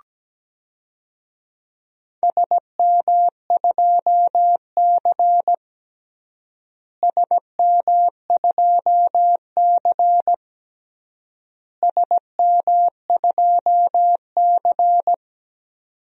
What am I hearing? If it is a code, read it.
Morse code, heard as a keyed tone, about 17 wpm, 700 Hz. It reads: SM2C SM2C SM2C
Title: DXCC-100 Call Signs - 4 Characters - Repeated 3x - 17wpm